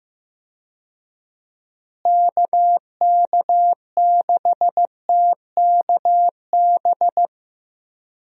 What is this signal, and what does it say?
Morse code, heard as a keyed tone, about 15 wpm, 700 Hz. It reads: KK6TKB
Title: USA Call Signs - 6 Characters - Repeated 3x - 15wpm